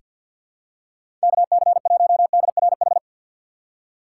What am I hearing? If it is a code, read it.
Morse code, heard as a keyed tone, about 46 wpm, 700 Hz. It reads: KC1DRH
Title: USA Call Signs - 6 Characters - 3x Character Spacing - 50wpm